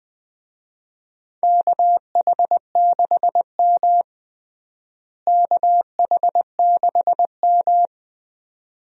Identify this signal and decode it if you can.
Morse code, heard as a keyed tone, about 20 wpm, 700 Hz. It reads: KH6M KH6M
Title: USA Call Signs - 4 Characters - Repeated 2x - 20wpm